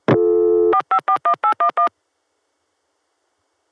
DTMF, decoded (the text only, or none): 4541811